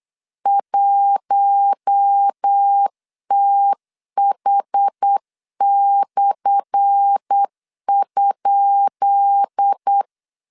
Morse code, heard as a keyed tone, about 8 wpm, 800 Hz. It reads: 1TH/?
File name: LEZIONI / 10 SIGNS / 10B INT 8wpm 3534